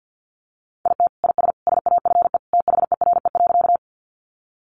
Morse code, heard as a keyed tone, about 50 wpm, 700 Hz. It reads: IT IS SAFE THERE.
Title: Easy Phrases 50wpm